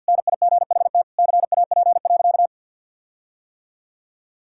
Morse code, heard as a keyed tone, about 50 wpm, 700 Hz. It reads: NIGHT CAP.
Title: Sentences from Top 1600 Words 50wpm